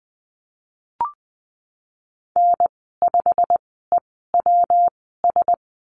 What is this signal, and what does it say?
Morse code, heard as a keyed tone, about 20 wpm, spaced as for 15 wpm, 700 Hz. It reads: N5EWS
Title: 179 Lesson 29g - Slash - Call Signs 20wpm